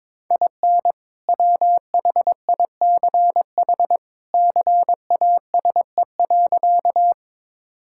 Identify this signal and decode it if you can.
Morse code, heard as a keyed tone, about 22 wpm, 700 Hz. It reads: IN WHICH CASE.
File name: practice / 22wpm / Sentences with 2-Word Phrases - 22wpm